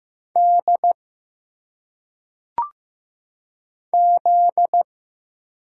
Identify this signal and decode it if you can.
Morse code, heard as a keyed tone, about 15 wpm, 700 Hz. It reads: D Z
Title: Single Letters - Phonetic Pronunciation 15wpm